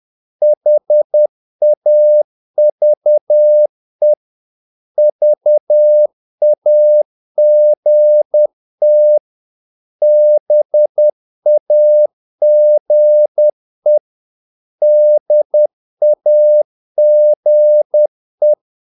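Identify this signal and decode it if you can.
Morse code, heard as a keyed tone, about 10 wpm, 600 Hz. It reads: HAVE VAGT BAGE DAGE
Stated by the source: Morse practice of short Danish words at 10 wpm, from CW med Gnister.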